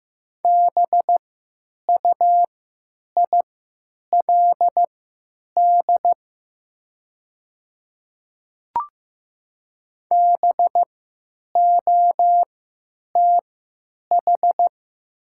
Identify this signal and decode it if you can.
Morse code, heard as a keyed tone, about 15 wpm, spaced as for 9 wpm, 690 Hz. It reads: BUILD BOTH
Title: 3x Character Spacing - Top 300 Words - 15wpm